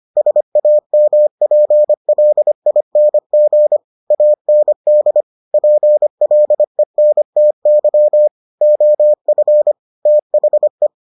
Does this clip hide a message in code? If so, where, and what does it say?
Yes: Morse code from 0.2 to 10.9 s, heard as a keyed tone, about 25 wpm, 600 Hz. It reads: SAMPLING AND PLENTY OF THE